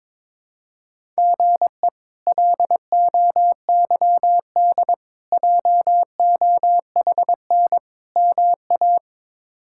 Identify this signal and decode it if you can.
Morse code, heard as a keyed tone, about 22 wpm, 700 Hz. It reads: GE LOYD JOHN MA
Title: K1USN SST QSO Practice - 22wpm